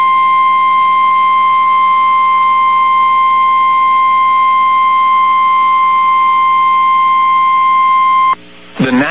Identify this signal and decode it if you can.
Emergency Alert System (SAME) tones (the 1050 Hz attention tone). It is heard from 0.4 to 8.2 s.
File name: eas received alerts / L1 19510 audio